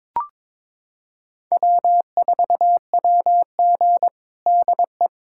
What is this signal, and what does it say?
Morse code, heard as a keyed tone, about 22 wpm, 700 Hz. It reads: W4WG DE
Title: USA Callsign DE Callsign - 4 Characters - 22wpm